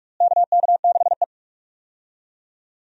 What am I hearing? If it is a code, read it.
Morse code, heard as a keyed tone, about 46 wpm, 700 Hz. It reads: KK6E
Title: USA Call Signs - 4 Characters - Repeated 3x - 45wpm